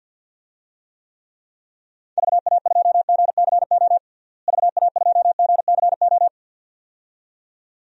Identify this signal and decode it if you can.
Morse code, heard as a keyed tone, about 53 wpm, 700 Hz. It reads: VA2GCO VA2GCO
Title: DXCC-100 Call Signs - 6 Characters - Repeated 2x - 50wpm